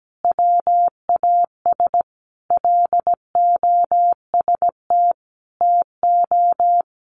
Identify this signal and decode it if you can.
Morse code, heard as a keyed tone, about 17 wpm, 700 Hz. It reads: WAS LOST TO
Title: Sentences from Top 1400 Words 17wpm